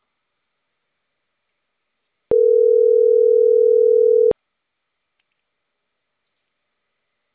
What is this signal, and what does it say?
Ringback tone (North America), 440 + 480 Hz.